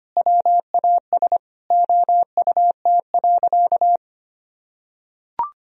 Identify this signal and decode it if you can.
Morse code, heard as a keyed tone, about 25 wpm, 700 Hz. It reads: WAS OUT.
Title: Sentences from Top 1000 Words - Review Part 4 - 25wpm